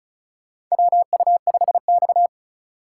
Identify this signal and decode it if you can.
Morse code, heard as a keyed tone, about 34 wpm, 700 Hz. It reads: WU5X
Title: USA Call Signs - 4 Characters - 2x Character Spacing - 35wpm